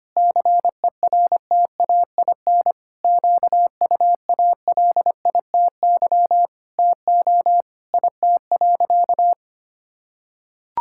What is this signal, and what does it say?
Morse code, heard as a keyed tone, about 25 wpm, 700 Hz. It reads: CERTAIN QUALITY TO IT.
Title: Sentences from Top 1000 Words - Review Part 6 - 25wpm